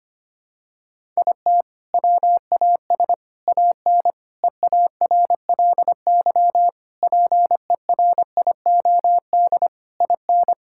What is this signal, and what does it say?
Morse code, heard as a keyed tone, about 25 wpm, 700 Hz. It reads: IT WAS AN EARLY PERIOD IN
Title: Sentences from Top 1000 Words - Review Part 5 - 25wpm